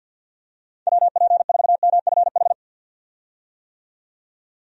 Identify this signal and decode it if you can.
Morse code, heard as a keyed tone, about 47 wpm, 700 Hz. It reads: WP4MFH
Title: USA Call Signs - 6 Characters - 50wpm